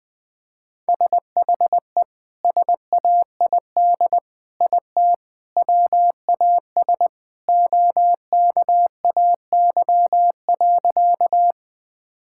Morse code, heard as a keyed tone, about 20 wpm, 700 Hz. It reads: SHE SAID IT WAS OKAY.
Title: Sentences with 2-Word Phrases - Spoken Beginning - 20wpm